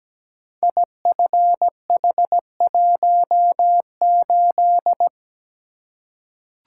Morse code, heard as a keyed tone, about 17 wpm, 700 Hz. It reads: IFH18
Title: Code Groups - 5 Letter:Numbers - 17wpm